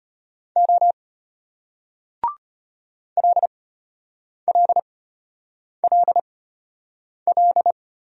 Morse code, heard as a keyed tone, about 33 wpm, spaced as for 8 wpm, 700 Hz. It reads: O LLLL